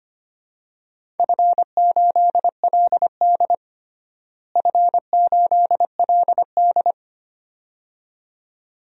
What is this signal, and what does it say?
Morse code, heard as a keyed tone, about 25 wpm, 700 Hz. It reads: F8LD F8LD
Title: DXCC-100 Call Signs - 4 Characters - Repeated 2x - 25wpm